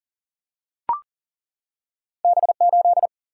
Morse code, heard as a keyed tone, about 40 wpm, 700 Hz. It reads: B8
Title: Single-Letter Single-Number 40wpm